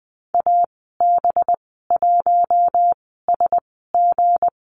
Morse code, heard as a keyed tone, about 20 wpm, spaced as for 14 wpm, 700 Hz. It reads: AB1SG
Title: 231 Lesson 37g - Letter x - Call Signs 20wpm